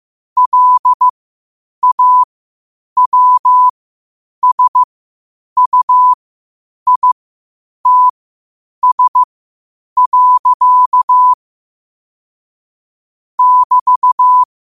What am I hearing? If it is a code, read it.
Morse code, heard as a keyed tone, about 15 wpm, spaced as for 8 wpm, 1000 Hz. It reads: LAWSUITS. <BT>
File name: qotdfiles / 8wpm